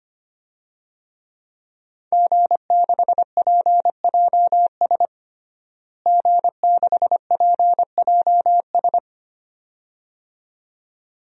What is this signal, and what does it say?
Morse code, heard as a keyed tone, about 25 wpm, 700 Hz. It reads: G6PJS G6PJS
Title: DXCC-100 Call Signs - 5 Characters - Repeated 2x - 25wpm